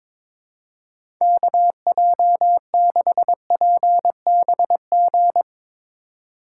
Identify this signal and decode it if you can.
Morse code, heard as a keyed tone, about 22 wpm, 700 Hz. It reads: KJ6PBG
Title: USA Call Signs - 6 Characters - 3x Character Spacing - 22wpm